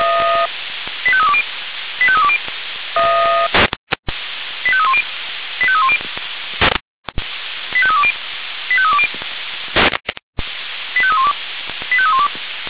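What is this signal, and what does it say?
Five-tone selective call (ZVEI), heard as quick runs of five tones, tones of about 70 ms. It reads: 85320 85320 85310 85310 85320 85320 85312 85312
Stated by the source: ZVEI five-tone call 85894 received by SDR.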